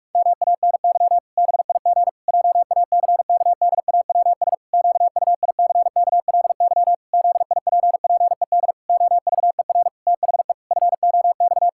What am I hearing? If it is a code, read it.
Morse code, heard as a keyed tone, about 48 wpm, 700 Hz. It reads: MANY BIG JACKDAWS QUICKLY ZIPPED OVER THE FOX
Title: Pangram Challenge v2 - 45wpm